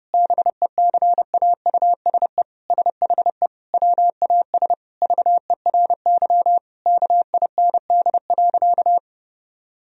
Morse code, heard as a keyed tone, about 30 wpm, 700 Hz. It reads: BECAUSE SHE WAS VERY KIND.
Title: Sentences from Top 1000 Words - Review Part 3 - 30wpm